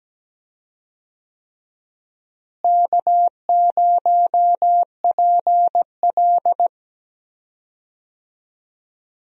Morse code, heard as a keyed tone, about 17 wpm, 700 Hz. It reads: K0PL
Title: USA Call Signs - 4 Characters - Repeated 3x - 17wpm